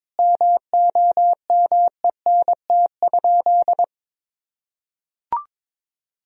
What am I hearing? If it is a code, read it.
Morse code, heard as a keyed tone, about 22 wpm, 700 Hz. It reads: MOMENT?
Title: Sentences from Top 1000 Words - Review Part 1 - 22wpm